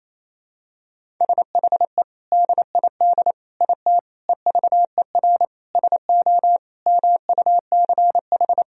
Morse code, heard as a keyed tone, about 28 wpm, 700 Hz. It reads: SHE DID IT EVER SO MUCH
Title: Sentences from Top 400 Words and Abbreviations 28wpm